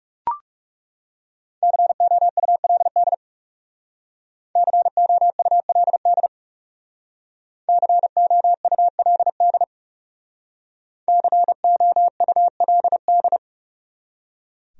Morse code, heard as a keyed tone, about 40 wpm, 700 Hz. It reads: COULD COULD COULD COULD